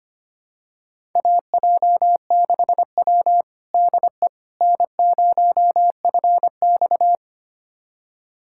Morse code, heard as a keyed tone, about 25 wpm, 700 Hz. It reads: AJ6W DE N0FX
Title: USA Callsign DE Callsign - 4 Characters - 25wpm